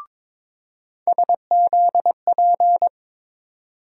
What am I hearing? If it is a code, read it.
Morse code, heard as a keyed tone, about 22 wpm, 700 Hz. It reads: SZP
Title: USA Call Sign Postfixes - 3 Letters 22wpm